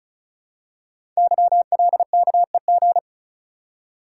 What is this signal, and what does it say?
Morse code, heard as a keyed tone, about 36 wpm, 700 Hz. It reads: YLKEG
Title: Code Groups - 5 Letters - 35wpm